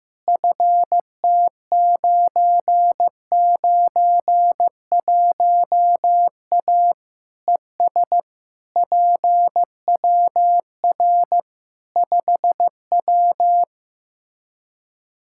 Morse code, heard as a keyed tone, about 14 wpm, 700 Hz. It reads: FT991A ES PWR 5W
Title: Practice QSO - Exchange 2 - Info - 15wpm